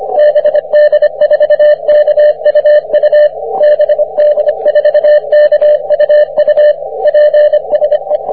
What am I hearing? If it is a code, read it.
Morse code, heard as a keyed tone, about 25 wpm, 600 Hz. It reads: BD4KUU BD4KUU PSE
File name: wp / bd4kuu 40m cw